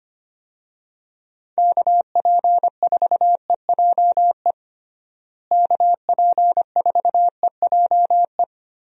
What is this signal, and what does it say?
Morse code, heard as a keyed tone, about 25 wpm, 700 Hz. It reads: KP4EJE KP4EJE
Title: DXCC-100 Call Signs - 6 Characters - Repeated 3x - 25wpm